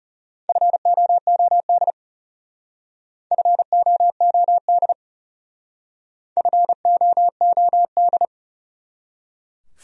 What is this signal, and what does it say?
Morse code, heard as a keyed tone, about 34 wpm, 700 Hz. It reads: FOOD FOOD FOOD
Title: Speed-Racing Top 500 Words 30wpm